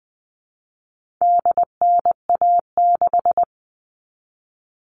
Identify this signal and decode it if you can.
Morse code, heard as a keyed tone, about 20 wpm, 700 Hz. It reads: DNA6
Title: All Cut Numbers - 4-Digit 20wpm